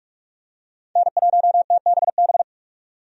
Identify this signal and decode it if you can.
Morse code, heard as a keyed tone, about 41 wpm, 700 Hz. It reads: N1TBB
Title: USA Call Signs - 5 Characters - 45wpm